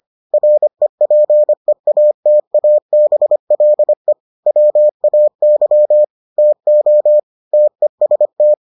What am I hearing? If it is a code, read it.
Morse code, heard as a keyed tone, about 25 wpm, 600 Hz. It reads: REPEATABLE WAY TO TEST